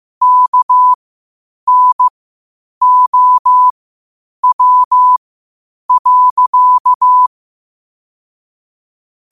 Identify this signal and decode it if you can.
Morse code, heard as a keyed tone, about 15 wpm, spaced as for 8 wpm, 1000 Hz. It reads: KNOW.
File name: qotdfiles / 8wpm